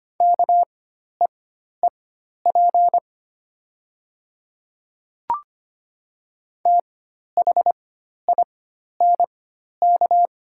Morse code, heard as a keyed tone, about 25 wpm, spaced as for 12 wpm, 700 Hz. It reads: KEEP THINK
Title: 4x Character Spacing - Top 300 Words - 25wpm